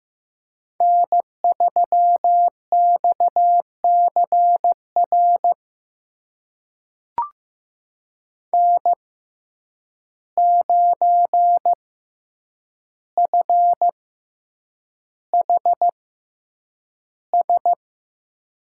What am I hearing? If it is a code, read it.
Morse code, heard as a keyed tone, about 15 wpm, spaced as for 8 wpm, 700 Hz. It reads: N3XCR N9FHS
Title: USA Call Signs - 5 Characters - 6x Character Spacing - 15wpm